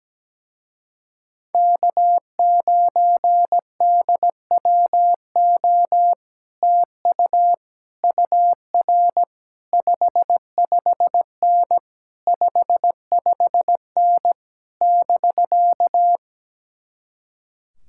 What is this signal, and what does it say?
Morse code, heard as a keyed tone, about 17 wpm, 700 Hz. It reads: K9DWO TU UR 55N 55N <BK>
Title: Speed-Racing Common QSO Phrases - 17wpm